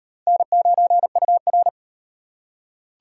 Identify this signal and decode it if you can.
Morse code, heard as a keyed tone, about 38 wpm, 700 Hz. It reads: N9UR